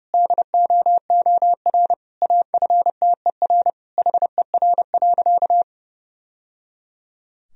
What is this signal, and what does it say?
Morse code, heard as a keyed tone, about 30 wpm, 700 Hz. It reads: DOOR AFTER HER.
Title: Sentences from Top 1000 Words - Review Part 1 - 30wpm